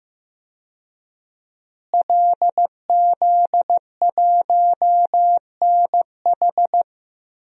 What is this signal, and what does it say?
Morse code, heard as a keyed tone, about 15 wpm, 700 Hz. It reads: LZ1NH
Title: DXCC-100 Call Signs - 5 Characters - Repeated 2x - 15wpm